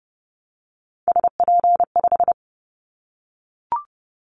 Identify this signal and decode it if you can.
Morse code, heard as a keyed tone, about 30 wpm, 700 Hz. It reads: SP5
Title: Callsign Prefixes for Top 25 DXCC - 30wpm